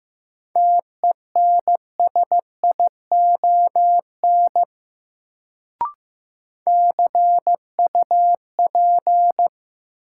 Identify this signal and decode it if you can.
Morse code, heard as a keyed tone, about 15 wpm, 700 Hz. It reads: TENSION CUP
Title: Sentences from Top 1600 Words - New Word Drill - 15wpm